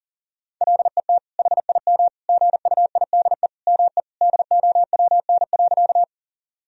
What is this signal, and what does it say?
Morse code, heard as a keyed tone, about 40 wpm, 700 Hz. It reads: LET HIM GUIDE ME DOWN.
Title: Sentences from Top 1000 Words - Review Part 3 - 40wpm